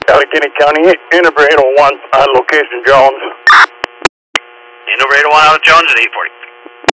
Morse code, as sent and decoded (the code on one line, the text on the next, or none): none